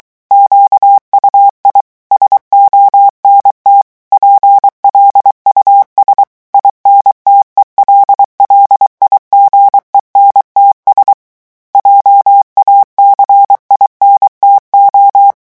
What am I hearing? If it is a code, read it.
Morse code, heard as a keyed tone, about 23 wpm, 800 Hz. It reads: QUI SONT PLUS INTELLIGENTS JACINTO
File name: morse mp3 / revision9 citations humoristiques vitesse 22 mots minute